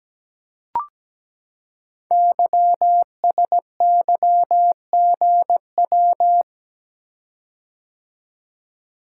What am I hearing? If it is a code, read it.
Morse code, heard as a keyed tone, about 17 wpm, 700 Hz. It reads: YSYGW
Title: Code Groups - 5 Letters - 17wpm